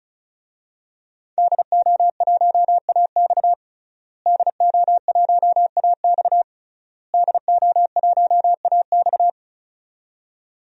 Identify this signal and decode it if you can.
Morse code, heard as a keyed tone, about 34 wpm, 700 Hz. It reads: DO1AX DO1AX DO1AX
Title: DXCC-100 Call Signs - 5 Characters - Repeated 3x - 35wpm